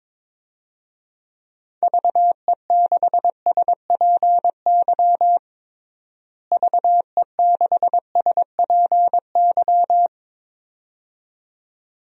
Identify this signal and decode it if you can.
Morse code, heard as a keyed tone, about 21 wpm, 700 Hz. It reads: VE6SPY VE6SPY
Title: DXCC-100 Call Signs - 6 Characters - Repeated 2x - 22wpm